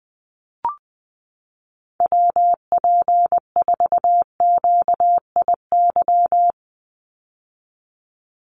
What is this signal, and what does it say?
Morse code, heard as a keyed tone, about 20 wpm, 700 Hz. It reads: WP4QIY
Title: USA Call Signs 20wpm